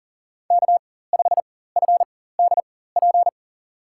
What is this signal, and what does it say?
Morse code, heard as a keyed tone, about 40 wpm, spaced as for 19 wpm, 700 Hz. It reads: K5FDP
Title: USA Call Signs - 5 Characters - 4x Character Spacing - 40wpm